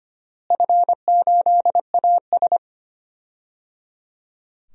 Morse code, heard as a keyed tone, about 25 wpm, 700 Hz. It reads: F8AS
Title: DXCC-100 Call Signs - 4 Characters - Repeated 3x - 25wpm